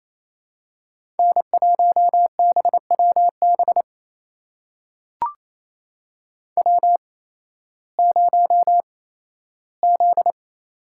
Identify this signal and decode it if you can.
Morse code, heard as a keyed tone, about 28 wpm, spaced as for 15 wpm, 700 Hz. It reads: N1BWB W0Z
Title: USA Call Signs - 5 Characters - 8x Character Spacing - 28wpm